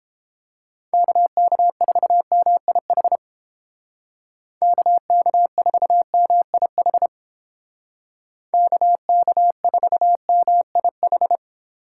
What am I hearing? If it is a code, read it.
Morse code, heard as a keyed tone, about 29 wpm, 700 Hz. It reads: KK4MIH KK4MIH KK4MIH